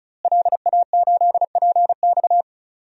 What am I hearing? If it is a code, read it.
Morse code, heard as a keyed tone, about 36 wpm, 700 Hz. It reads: LA8PX
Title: DXCC-100 Call Signs - 5 Characters - Repeated 3x - 35wpm